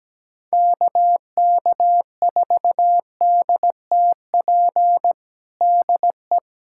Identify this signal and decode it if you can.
Morse code, heard as a keyed tone, about 17 wpm, 700 Hz. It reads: KK4DTP DE
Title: USA Callsign DE Callsign - 6 Characters - 17wpm